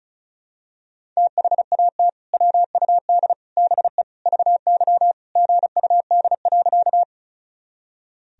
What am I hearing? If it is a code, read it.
Morse code, heard as a keyed tone, about 36 wpm, 700 Hz. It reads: THAT WUD BE VY GUD.
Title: Sentences from Top 200 Words and Abbreviations 35wpm